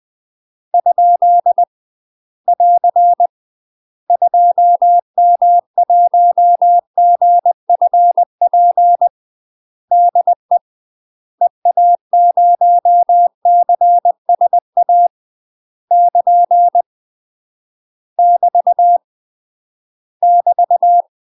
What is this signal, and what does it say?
Morse code, heard as a keyed tone, about 20 wpm, 700 Hz. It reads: ? <AR> 2M1GFP DE EA0CSA <KN> <BT> <BT>
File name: UK MASTER QSOs TRAINING/20WPM